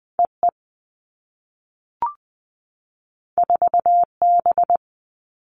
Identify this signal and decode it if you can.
Morse code, heard as a keyed tone, about 19 wpm, 700 Hz. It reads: E E 4 B
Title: Two Challenging Letter-Number Combinations 20wpm